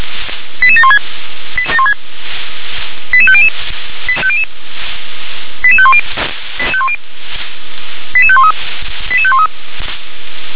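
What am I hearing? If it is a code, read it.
Five-tone selective call (ZVEI), heard as quick runs of five tones, tones of about 70 ms. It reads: 88716 88500 88510 88513 88513
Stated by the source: ZVEI five-tone call 88758 received by SDR.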